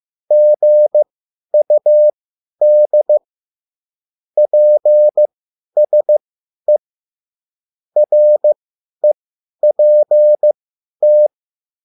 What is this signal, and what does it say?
Morse code, heard as a keyed tone, about 15 wpm, spaced as for 10 wpm, 600 Hz. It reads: GUD PSE REPT
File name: QSL / CW lesson